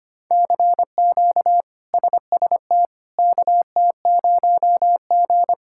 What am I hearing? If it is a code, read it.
Morse code, heard as a keyed tone, about 25 wpm, 700 Hz. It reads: CQ SST KT0G
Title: K1USN SST QSO Practice - 25wpm